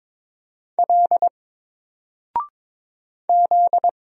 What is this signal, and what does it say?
Morse code, heard as a keyed tone, about 22 wpm, 700 Hz. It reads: L Z
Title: Single Letter-Number 22wpm